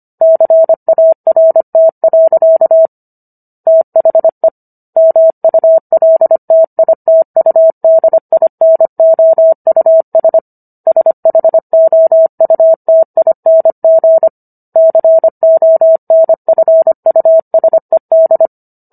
Morse code, heard as a keyed tone, about 25 wpm, 650 Hz. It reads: CART. THE MULTITUDINOUS SHOUTING CONFUSED